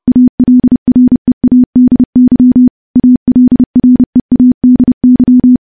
Morse code, heard as a keyed tone, about 30 wpm, 250 Hz. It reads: ALREADY ALREADY
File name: morse / morse-training-30WPM